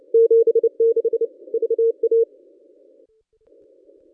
Morse code, heard as a keyed tone, about 29 wpm, 460 Hz. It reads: 76 VA
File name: contests / ss-cw-k5tr-op-n5rz -1200